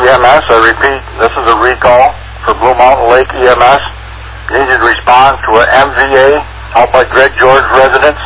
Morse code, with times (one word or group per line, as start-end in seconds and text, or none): none